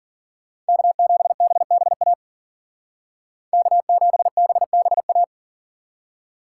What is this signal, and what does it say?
Morse code, heard as a keyed tone, about 49 wpm, 700 Hz. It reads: K7BBA K7BBA
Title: Speed Racing - USA 1x3 Call Signs 40wpm